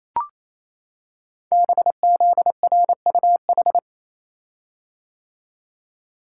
Morse code, heard as a keyed tone, about 27 wpm, 700 Hz. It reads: BZRUH
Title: Code Groups - 5 Letters - 28wpm